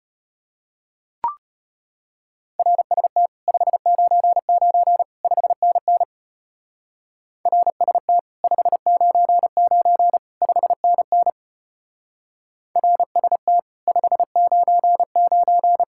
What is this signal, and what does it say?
Morse code, heard as a keyed tone, about 32 wpm, 700 Hz. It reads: RST 599 5NN RST 599 5NN RST 599